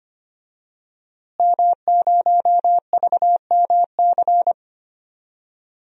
Morse code, heard as a keyed tone, about 25 wpm, 700 Hz. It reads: M0VMC
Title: DXCC-100 Call Signs - 5 Characters - Repeated 2x - 25wpm